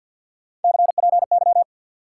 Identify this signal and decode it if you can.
Morse code, heard as a keyed tone, about 62 wpm, 700 Hz. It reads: CPY
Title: Speed-Racing Top 180 QSO Elements 50wpm